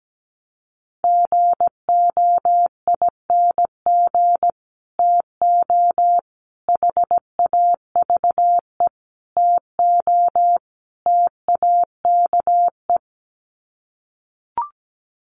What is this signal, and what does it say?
Morse code, heard as a keyed tone, about 17 wpm, 700 Hz. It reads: GOING TO HAVE TO TAKE
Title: Sets of 5 Words Encore 17wpm